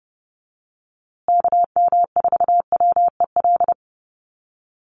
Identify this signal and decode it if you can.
Morse code, heard as a keyed tone, about 30 wpm, 700 Hz. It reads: KM4WEL
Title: USA Call Signs Repeated 2x 30wpm